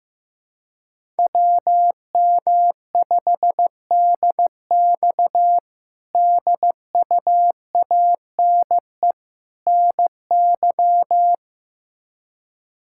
Morse code, heard as a keyed tone, about 16 wpm, 700 Hz. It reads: WM5DX DUANE NY
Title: K1USN SST QSO Practice - 15wpm